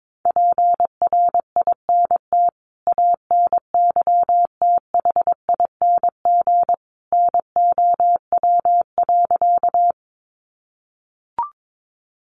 Morse code, heard as a keyed tone, about 22 wpm, 700 Hz. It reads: PRINT ANYTHING NOW.
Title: Sentences from Top 1400 Words 22wpm